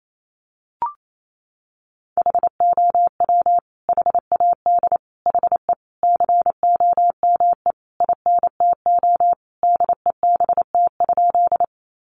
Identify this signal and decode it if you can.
Morse code, heard as a keyed tone, about 28 wpm, 700 Hz. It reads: HOW HAD HE COME INTO DEBT?
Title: Sentences from Top 1300 Words 28wpm